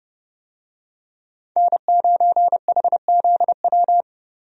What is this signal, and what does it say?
Morse code, heard as a keyed tone, about 30 wpm, 700 Hz. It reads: N9HZW
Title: USA Call Signs - 5 Characters - 30wpm